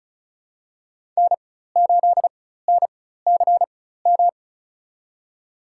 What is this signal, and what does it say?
Morse code, heard as a keyed tone, about 36 wpm, spaced as for 16 wpm, 700 Hz. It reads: N8NCM
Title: USA Call Signs - 5 Characters - 4x Character Spacing - 35wpm